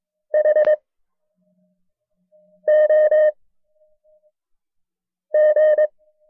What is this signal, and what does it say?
Morse code, heard as a keyed tone, about 22 wpm, 610 Hz. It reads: H O G
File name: FreeFiles / CW-practice / KB6MT-tape1-side2